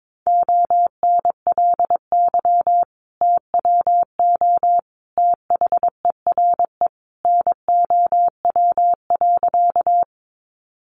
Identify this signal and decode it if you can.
Morse code, heard as a keyed tone, about 22 wpm, 700 Hz. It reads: ONLY TWO THERE NOW.
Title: Sentences from Top 200 Words 22wpm